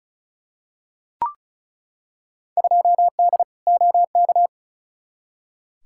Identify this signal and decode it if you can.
Morse code, heard as a keyed tone, about 36 wpm, 700 Hz. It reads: 2D OK
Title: Field Day Exchanges Encore - 35wpm